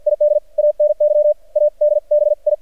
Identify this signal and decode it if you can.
Morse code, heard as a keyed tone, about 46 wpm, 600 Hz, with radio noise. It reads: ID ANY ADBI
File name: archived / CRSnet